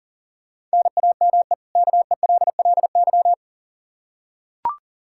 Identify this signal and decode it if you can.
Morse code, heard as a keyed tone, about 38 wpm, 700 Hz. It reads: NAME KELLY
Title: K1USN SST QSO Practice 40wpm